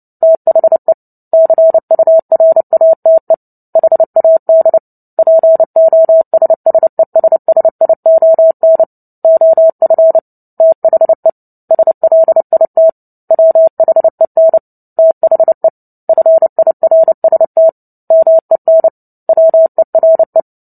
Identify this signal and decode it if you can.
Morse code, heard as a keyed tone, about 30 wpm, 650 Hz. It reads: THE CURATE HAD POSSESSION OF THE SLIT WHEN THE FIRST MEN WERE
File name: Book Two / 03-Chapter 3 - 29 WPM